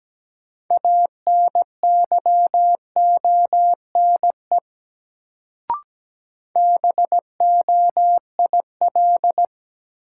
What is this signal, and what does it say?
Morse code, heard as a keyed tone, about 17 wpm, 700 Hz. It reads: ANYONE BOIL
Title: Sets of 5 Words Encore - New Word Drill - 17wpm